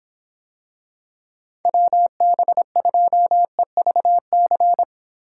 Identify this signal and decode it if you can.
Morse code, heard as a keyed tone, about 25 wpm, 700 Hz. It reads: WB2EVC